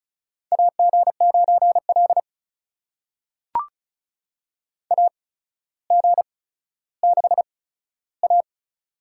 Morse code, heard as a keyed tone, about 36 wpm, spaced as for 15 wpm, 700 Hz. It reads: AG9L AG6A
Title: USA Call Signs - 4 Characters - 8x Character Spacing - 35wpm